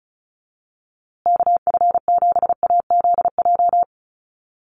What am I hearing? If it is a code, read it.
Morse code, heard as a keyed tone, about 34 wpm, 700 Hz. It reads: KF7AZJ
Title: USA Call Signs Repeated 3x 35wpm